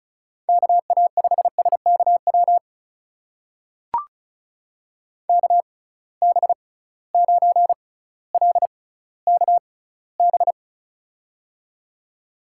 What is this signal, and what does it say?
Morse code, heard as a keyed tone, about 36 wpm, spaced as for 18 wpm, 700 Hz. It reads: KA5SKW KB9LKB
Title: USA Call Signs - 6 Characters - 6x Character Spacing - 35wpm